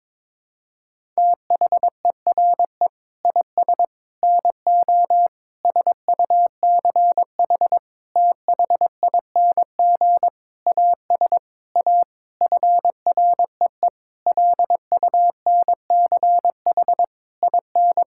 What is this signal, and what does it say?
Morse code, heard as a keyed tone, about 22 wpm, 700 Hz. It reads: THERE IS NO SUCH THING AS A FREE LUNCH IN
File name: Sentences with 9-Word Phrases - 22wpm